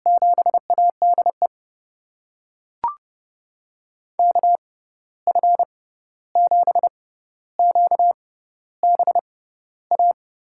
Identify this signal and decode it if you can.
Morse code, heard as a keyed tone, about 30 wpm, spaced as for 14 wpm, 700 Hz. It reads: 7ADE KF7QBA
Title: USA Call Signs - 6 Characters - 6x Character Spacing - 30wpm